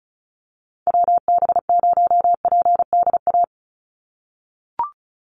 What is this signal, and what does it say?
Morse code, heard as a keyed tone, about 34 wpm, 700 Hz. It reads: WB0PDA
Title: USA Call Signs 35wpm